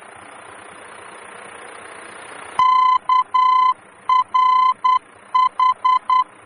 Morse code, heard as a keyed tone, about 10 wpm, 1020 Hz. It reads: KRH
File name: germany / mx / KRH-D-VOR